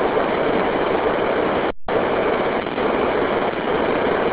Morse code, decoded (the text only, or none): none